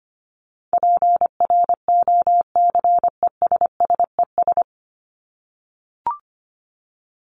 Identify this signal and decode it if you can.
Morse code, heard as a keyed tone, about 25 wpm, 700 Hz. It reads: PROCESSES
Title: ES:S Suffix - Top 100 Words with Base Words 25wpm